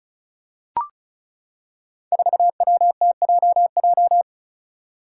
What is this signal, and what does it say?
Morse code, heard as a keyed tone, about 36 wpm, 700 Hz. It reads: 4WTJJ
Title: Five Challenging Letter-Number Combinations 35wpm